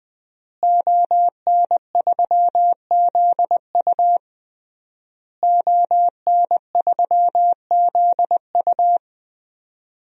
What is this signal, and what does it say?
Morse code, heard as a keyed tone, about 20 wpm, 700 Hz. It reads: ON3ZU ON3ZU
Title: DXCC-100 Call Signs - 5 Characters - Repeated 3x - 20wpm